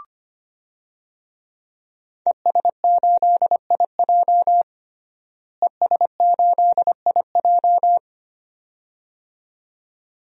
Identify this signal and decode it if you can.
Morse code, heard as a keyed tone, about 25 wpm, 700 Hz. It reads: ES8IJ ES8IJ
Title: DXCC-100 Call Signs - 5 Characters - Repeated 2x - 25wpm